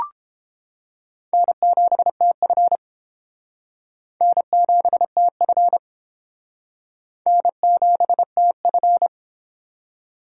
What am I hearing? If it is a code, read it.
Morse code, heard as a keyed tone, about 31 wpm, 700 Hz. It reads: N7TF N7TF N7TF